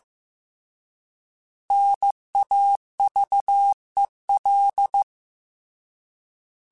Morse code, heard as a keyed tone, about 15 wpm, 800 Hz. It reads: NAVEL
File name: Alphabetical / Alphabetical mp3s / Ns @13